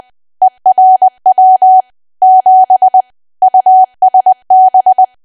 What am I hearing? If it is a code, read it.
Morse code, heard as a keyed tone, about 20 wpm, 750 Hz. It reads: ERW 7 USB
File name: Archive / 20 WPM / 20WPM